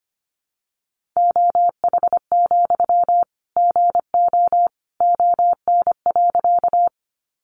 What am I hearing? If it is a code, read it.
Morse code, heard as a keyed tone, about 25 wpm, 700 Hz. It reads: OH, GO ON.
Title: Sentences from Top 500 Words 25wpm